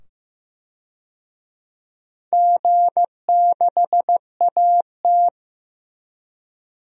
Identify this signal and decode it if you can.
Morse code, heard as a keyed tone, about 15 wpm, 700 Hz. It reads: G6AT
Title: DXCC-100 Call Signs - 4 Characters - 15wpm